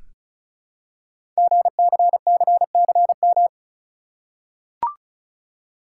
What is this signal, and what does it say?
Morse code, heard as a keyed tone, about 36 wpm, 690 Hz. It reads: GCCCM
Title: Code Groups - 5 Letters - 35wpm